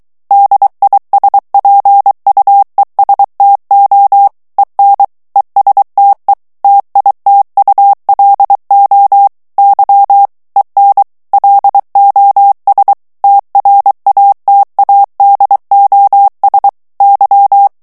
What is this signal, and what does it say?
Morse code, heard as a keyed tone, about 23 wpm, 800 Hz. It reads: DISPUESTO EN ESTE TITULO Y EN LOS TRATADOS Y
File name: ESPAGNOL TEXTE EN CLAIR 2 VITESSE 22 MOTS MINUTES